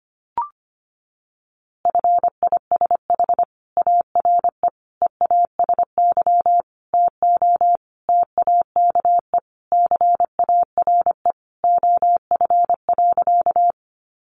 Morse code, heard as a keyed tone, about 25 wpm, 700 Hz. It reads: FISH ARE EASY TO TAKE CARE OF.
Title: Easy Phrases 25wpm